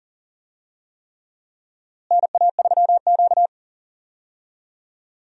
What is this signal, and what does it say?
Morse code, heard as a keyed tone, about 40 wpm, 700 Hz. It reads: NA3Q
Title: USA Call Signs - 4 Characters - Repeated 3x - 40wpm